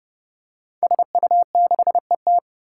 Morse code, heard as a keyed tone, about 30 wpm, 700 Hz. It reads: SU6ET
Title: Five Challenging Letter-Number Combinations 30wpm